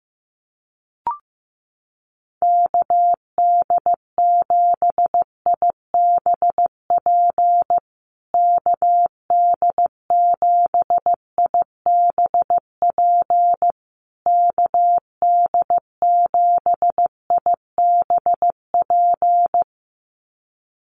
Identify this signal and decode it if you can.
Morse code, heard as a keyed tone, about 15 wpm, 700 Hz. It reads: KD7IBP KD7IBP KD7IBP
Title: USA Call Signs Repeated 3x 15wpm